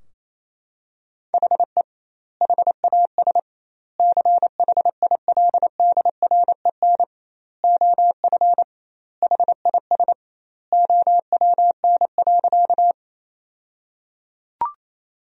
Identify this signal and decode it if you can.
Morse code, heard as a keyed tone, about 27 wpm, 700 Hz. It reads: HE HAS CHILDREN OF HIS OWN.
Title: 2x Word-Spacing - Sentences from Top 300 Words - 28wpm